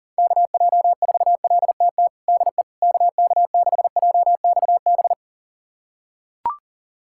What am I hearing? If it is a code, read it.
Morse code, heard as a keyed tone, about 40 wpm, 700 Hz. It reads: KJ4LTT DE KK6JXB
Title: USA Callsign DE Callsign 40wpm